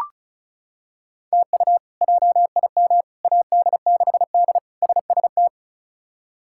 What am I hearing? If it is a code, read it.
Morse code, heard as a keyed tone, about 34 wpm, 700 Hz. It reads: TU JIM AD6D SST
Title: K1USN SST QSO Practice - 35wpm